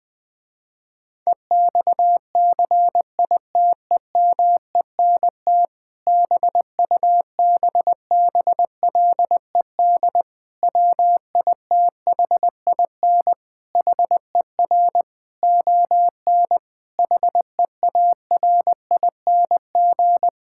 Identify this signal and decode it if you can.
Morse code, heard as a keyed tone, about 20 wpm, 700 Hz. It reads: EXCITEMENT BUBBLED WITHIN HER ON HEARING
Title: Difficult Sentences with Hints - 20wpm